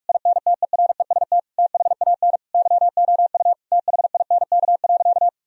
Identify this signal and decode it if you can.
Morse code, heard as a keyed tone, about 44 wpm, 700 Hz. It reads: INTEREST THAN YOU THINK.
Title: Sentences from Top 1000 Words - Review Part 2 - 45wpm